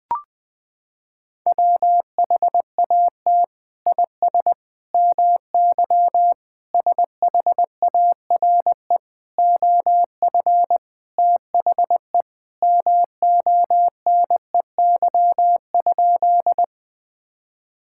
Morse code, heard as a keyed tone, about 20 wpm, 700 Hz. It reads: WHAT IS MY SHARE OF THE MONEY?
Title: Sentences from Top 1000 Words - Review Part 6 - 20wpm